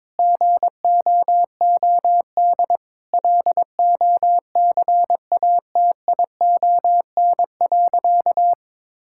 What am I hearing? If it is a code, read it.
Morse code, heard as a keyed tone, about 22 wpm, 700 Hz. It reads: GOOD LOCATION.
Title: Sentences from Top 1000 Words - Review Part 6 - 22wpm